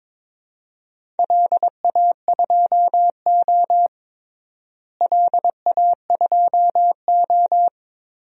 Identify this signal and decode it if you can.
Morse code, heard as a keyed tone, about 22 wpm, 700 Hz. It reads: LA2O LA2O
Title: DXCC-100 Call Signs - 4 Characters - Repeated 2x - 22wpm